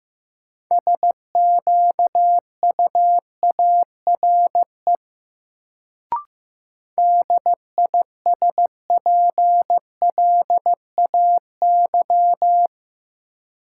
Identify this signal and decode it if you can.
Morse code, heard as a keyed tone, about 15 wpm, 700 Hz. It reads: SQUARE DISPLAY
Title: Sentences from Top 1200 Words - New Word Drill Alt - 15wpm